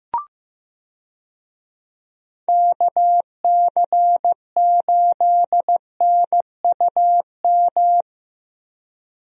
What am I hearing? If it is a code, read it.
Morse code, heard as a keyed tone, about 15 wpm, 700 Hz. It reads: KC8NUM
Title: USA Call Signs - 6 Characters - Repeated 3x - 15wpm